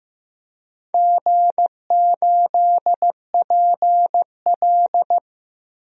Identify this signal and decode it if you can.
Morse code, heard as a keyed tone, about 15 wpm, 700 Hz. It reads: G8PL
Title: DXCC-100 Call Signs - 4 Characters - Repeated 3x - 15wpm